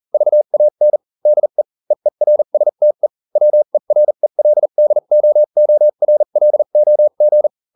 Morse code, heard as a keyed tone, about 40 wpm, spaced as for 35 wpm, 600 Hz. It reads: VAN DE EERSTE WERELDOORLOG